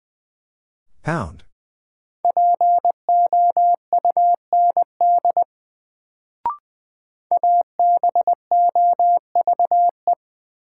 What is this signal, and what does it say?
Morse code, heard as a keyed tone, about 20 wpm, 700 Hz. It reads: POUND ABOVE
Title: Sentences from Top 1000 Words - Review Part 2 - New Word Drill - 20wpm